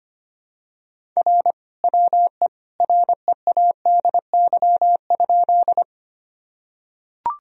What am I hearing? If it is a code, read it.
Morse code, heard as a keyed tone, about 25 wpm, 700 Hz. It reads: R WE READY?
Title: Sentences from Top 400 Words and Abbreviations 25wpm